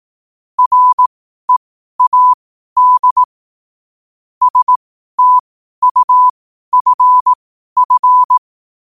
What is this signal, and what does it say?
Morse code, heard as a keyed tone, about 18 wpm, spaced as for 12 wpm, 1000 Hz. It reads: READ STUFF
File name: 12wpm